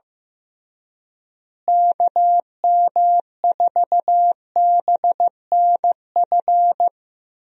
Morse code, heard as a keyed tone, about 16 wpm, 700 Hz. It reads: KM4BNF
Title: USA Call Signs - 6 Characters - Repeated 3x - 15wpm